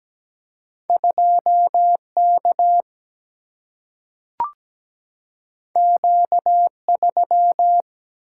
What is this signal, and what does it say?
Morse code, heard as a keyed tone, about 17 wpm, 700 Hz. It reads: 2K Q3
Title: Single-Letter Single-Number 17wpm